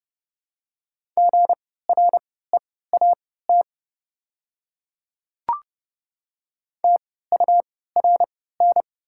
Morse code, heard as a keyed tone, about 30 wpm, spaced as for 17 wpm, 700 Hz. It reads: GREAT TURN
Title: 3x Character Spacing - Top 300 Words - 30wpm